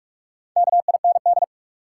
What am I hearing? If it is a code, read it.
Morse code, heard as a keyed tone, about 43 wpm, 700 Hz. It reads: KIND
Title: Sets of 2 Words - New Word Drill - 45wpm